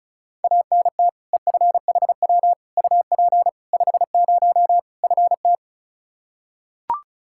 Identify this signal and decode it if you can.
Morse code, heard as a keyed tone, about 36 wpm, 700 Hz. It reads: ANT EFHW UP 50 FT
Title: Practice QSO - Exchange 2 - Info - 35wpm